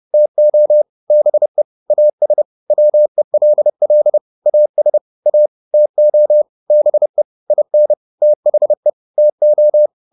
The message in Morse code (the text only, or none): TO BE AS WELL AS A TO BE IN THE TO